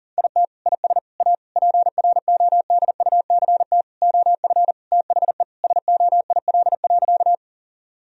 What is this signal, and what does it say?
Morse code, heard as a keyed tone, about 40 wpm, 700 Hz. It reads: IT IS A PRODUCT OF THE SOIL.
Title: Sentences from Top 1000 Words - Review Part 4 - 40wpm